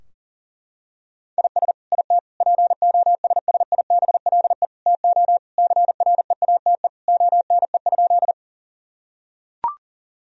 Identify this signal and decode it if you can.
Morse code, heard as a keyed tone, about 37 wpm, 700 Hz. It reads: IS IT POSSIBLE TO CREATE ONE?
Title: Sentences from Top 1000 Words - Review Part 3 - 40wpm